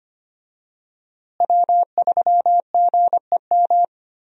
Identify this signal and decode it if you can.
Morse code, heard as a keyed tone, about 25 wpm, 700 Hz. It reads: W3GEM
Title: USA Call Signs - 5 Characters - Repeated 2x - 25wpm